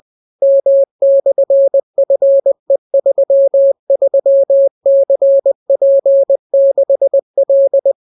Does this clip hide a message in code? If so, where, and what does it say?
Yes: Morse code from 0.4 to 7.9 s, heard as a keyed tone, about 20 wpm, 550 Hz. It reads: M/FE33CP6L